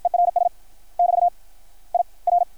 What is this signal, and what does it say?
Morse code, heard as a keyed tone, about 53 wpm, 690 Hz. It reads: ERS <BT> I R
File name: archived / CRSnet